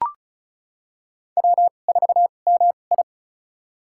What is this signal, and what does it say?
Morse code, heard as a keyed tone, about 34 wpm, spaced as for 25 wpm, 700 Hz. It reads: W4MI
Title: USA Call Signs - 2x Character Spacing - 35wpm